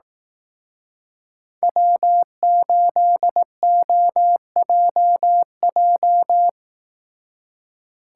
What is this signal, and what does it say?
Morse code, heard as a keyed tone, about 18 wpm, 700 Hz. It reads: W8OJJ
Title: Speed Racing - USA 1x3 Call Signs 15wpm